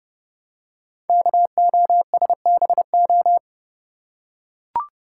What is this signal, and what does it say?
Morse code, heard as a keyed tone, about 30 wpm, 700 Hz. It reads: KOSBO
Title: Code Groups - 5 Letters - 30wpm